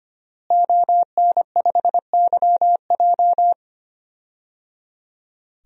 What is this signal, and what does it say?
Morse code, heard as a keyed tone, about 25 wpm, 700 Hz. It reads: ON5YJ
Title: DXCC-100 Call Signs - 5 Characters - Repeated 3x - 25wpm